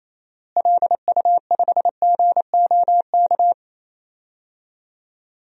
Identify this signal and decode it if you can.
Morse code, heard as a keyed tone, about 28 wpm, 700 Hz. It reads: LU5GOK
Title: DXCC-100 Call Signs - 6 Characters - Repeated 3x - 28wpm